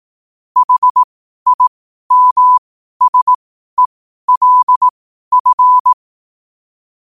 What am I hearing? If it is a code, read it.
Morse code, heard as a keyed tone, about 18 wpm, spaced as for 12 wpm, 1000 Hz. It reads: HIMSELF
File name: qotdfiles / 12wpm